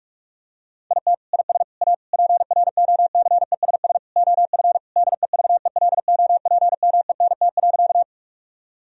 Morse code, heard as a keyed tone, about 41 wpm, 700 Hz. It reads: IT IS A PROCESS OF DEVELOPMENT.
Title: Sentences from Top 1000 Words - Review Part 4 - 45wpm